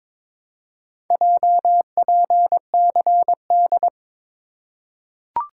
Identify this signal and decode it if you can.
Morse code, heard as a keyed tone, about 22 wpm, 700 Hz. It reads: JPCD
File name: practice / 22wpm / Four Letter-Number Combinations - 22wpm